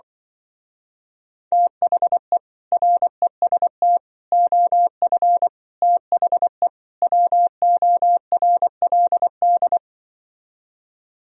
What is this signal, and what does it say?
Morse code, heard as a keyed tone, about 24 wpm, 700 Hz. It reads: THE REST OF THE WORLD
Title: Speed-Racing - Sets of 5 Words 20wpm